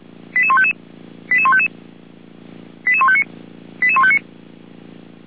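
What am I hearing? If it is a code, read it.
Five-tone selective call (ZVEI), heard as quick runs of five tones, tones of about 80 ms. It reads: 88155 88155 88159 88159
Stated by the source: ZVEI five-tone call 88080 received by SDR.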